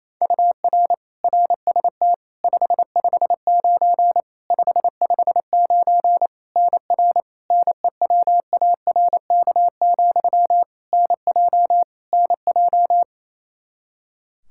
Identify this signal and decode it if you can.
Morse code, heard as a keyed tone, about 28 wpm, 700 Hz. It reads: UR RST 559 559 NR NEWARK, NJ NJ
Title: K1USN SST QSO Practice 28wpm